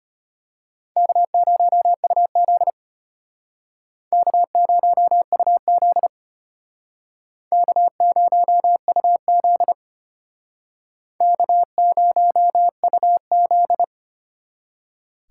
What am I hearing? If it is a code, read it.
Morse code, heard as a keyed tone, about 32 wpm, 700 Hz. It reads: K0UZ K0UZ K0UZ K0UZ